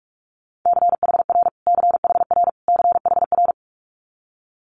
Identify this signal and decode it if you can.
Morse code, heard as a keyed tone, about 46 wpm, 700 Hz. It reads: CHR CHR CHR
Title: Multi-letter Phonemes Repeated 3x 45wpm